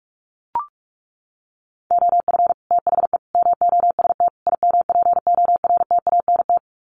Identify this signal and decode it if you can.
Morse code, heard as a keyed tone, about 44 wpm, 700 Hz. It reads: OF THE MOST IMPORTANT
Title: Sets of 4 Words 45wpm